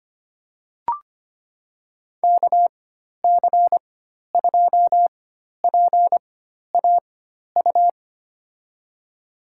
Morse code, heard as a keyed tone, about 25 wpm, spaced as for 12 wpm, 700 Hz. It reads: KC2PAU
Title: USA Call Signs - 6 Characters - 4x Character Spacing - 25wpm